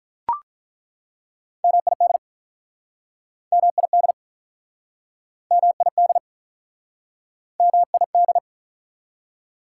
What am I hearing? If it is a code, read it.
Morse code, heard as a keyed tone, about 47 wpm, 700 Hz. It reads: MID MID MID MID